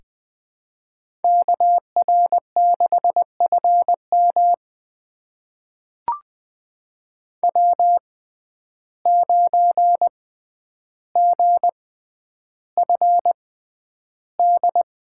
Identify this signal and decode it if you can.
Morse code, heard as a keyed tone, about 20 wpm, spaced as for 10 wpm, 700 Hz. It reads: KR6FM W9GFD
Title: USA Call Signs - 5 Characters - 6x Character Spacing - 20wpm